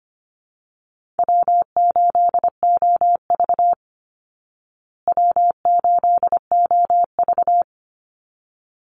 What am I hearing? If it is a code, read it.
Morse code, heard as a keyed tone, about 25 wpm, 700 Hz. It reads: W8OV W8OV
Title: Easy Call Signs Repeated 2x - 25wpm